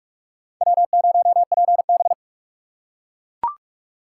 Morse code, heard as a keyed tone, about 43 wpm, 700 Hz. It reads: W0PB
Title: USA Call Signs - 4x Character Spacing - 45wpm